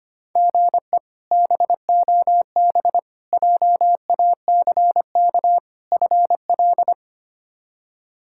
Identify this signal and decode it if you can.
Morse code, heard as a keyed tone, about 25 wpm, 700 Hz. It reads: GE BOB JACK FL
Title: K1USN SST QSO Practice - 25wpm